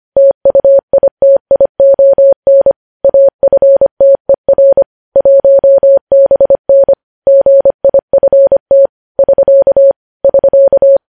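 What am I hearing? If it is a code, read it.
Morse code, heard as a keyed tone, about 25 wpm, 550 Hz. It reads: TUITION AFTER 1BN GIFT <SK> <SK>
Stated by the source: Morse code news headlines sent at 25 wpm.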